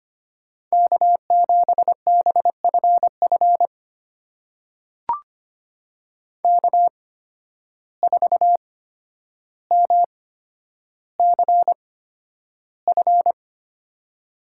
Morse code, heard as a keyed tone, about 25 wpm, spaced as for 11 wpm, 700 Hz. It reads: K7BFF K4MCF
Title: USA Call Signs - 5 Characters - 8x Character Spacing - 25wpm